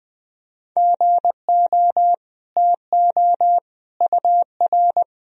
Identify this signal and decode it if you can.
Morse code, heard as a keyed tone, about 20 wpm, 700 Hz. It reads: GO TO UR
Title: Sentences from Top 300 Words and Abbreviations 20wpm